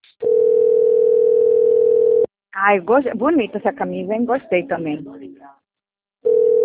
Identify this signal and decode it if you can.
Ringback tone (North America), 440 + 480 Hz, with speech.
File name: rg-1600-unknown- - - .6745